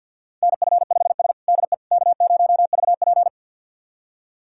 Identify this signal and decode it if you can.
Morse code, heard as a keyed tone, about 47 wpm, 700 Hz. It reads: NF5S DE K0VP
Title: USA Callsign DE Callsign - 4 Characters - 50wpm